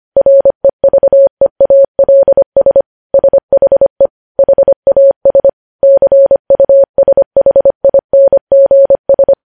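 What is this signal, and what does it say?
Morse code, heard as a keyed tone, about 25 wpm, 550 Hz. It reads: REVEALS SHE HAS CUSHINGS
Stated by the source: News headlines sent at 25.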